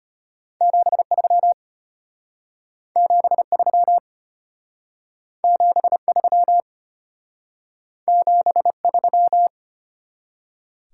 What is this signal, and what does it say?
Morse code, heard as a keyed tone, about 31 wpm, 700 Hz. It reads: 73 73 73 73
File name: practice / 25wpm / Speed-Racing Top 20 QSO Elements - 25wpm